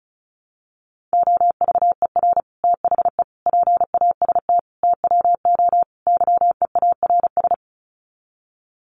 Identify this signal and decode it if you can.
Morse code, heard as a keyed tone, about 34 wpm, 700 Hz. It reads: OVER THE PAST TWO YEARS
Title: Sets of 5 Words Encore 35wpm